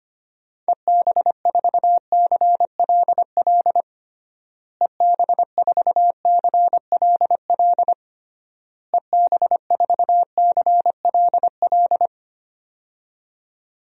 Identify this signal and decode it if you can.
Morse code, heard as a keyed tone, about 25 wpm, 700 Hz. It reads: EB4CLL EB4CLL EB4CLL
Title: DXCC-100 Call Signs - 6 Characters - Repeated 3x - 25wpm